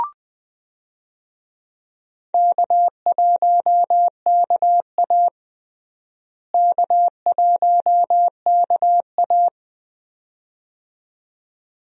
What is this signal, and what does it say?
Morse code, heard as a keyed tone, about 20 wpm, 700 Hz. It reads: K1KA K1KA
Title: USA Call Signs - 4 Characters - Repeated 2x - 20wpm